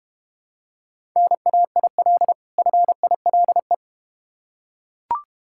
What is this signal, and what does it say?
Morse code, heard as a keyed tone, about 31 wpm, 700 Hz. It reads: NAIL FILE
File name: practice / 25wpm / Difficult Words - Speed-Racing with Hints - 25wpm